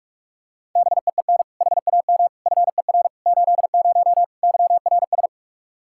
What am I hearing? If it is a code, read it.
Morse code, heard as a keyed tone, about 46 wpm, 700 Hz. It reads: BEEN HAM FER 80 YRS
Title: K1USN SST QSO Practice 45wpm